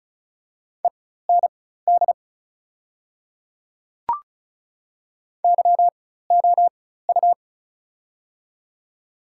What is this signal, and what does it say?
Morse code, heard as a keyed tone, about 34 wpm, spaced as for 16 wpm, 690 Hz. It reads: END YOU
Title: 4x Character Spacing - Top 300 Words - 35wpm